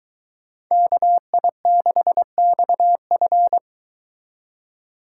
Morse code, heard as a keyed tone, about 22 wpm, 700 Hz. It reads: KI6XF